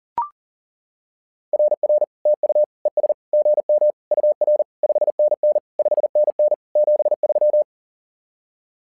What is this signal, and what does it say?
Morse code, heard as a keyed tone, about 40 wpm, 600 Hz. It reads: RR TU ES GM UR 5NN 5NN 73
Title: SOTA QSO Practice 40wpm